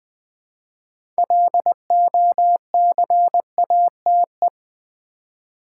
Morse code, heard as a keyed tone, about 20 wpm, 700 Hz. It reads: LOCATE
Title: 6 Letter Words - First 3 Letters Given - 20wpm